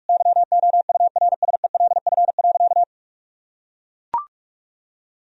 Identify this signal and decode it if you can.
Morse code, heard as a keyed tone, about 41 wpm, 700 Hz. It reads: YOURSELF.
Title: Sentences from Top 1000 Words - Review Part 2 - 45wpm